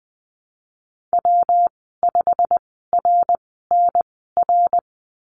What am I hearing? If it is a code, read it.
Morse code, heard as a keyed tone, about 20 wpm, spaced as for 15 wpm, 700 Hz. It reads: W5RNR
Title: 167 Lesson 27g - Number 7 - Call Signs 20wpm